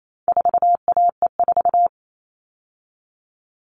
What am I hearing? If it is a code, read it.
Morse code, heard as a keyed tone, about 28 wpm, 700 Hz. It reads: 4AE4
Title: All Cut Numbers - 4-Digit 28wpm